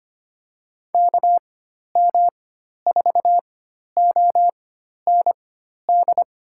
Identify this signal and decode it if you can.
Morse code, heard as a keyed tone, about 25 wpm, spaced as for 12 wpm, 700 Hz. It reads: KM4OND
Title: USA Call Signs - 4x Character Spacing - 25wpm